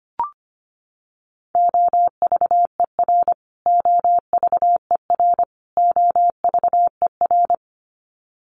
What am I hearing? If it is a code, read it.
Morse code, heard as a keyed tone, about 25 wpm, 700 Hz. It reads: OVER OVER OVER
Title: Top 200 Words Repeated 25wpm